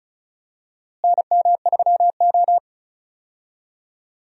Morse code, heard as a keyed tone, about 36 wpm, 700 Hz. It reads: NM3O
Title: USA Call Signs - 4 Characters - Repeated 3x - 35wpm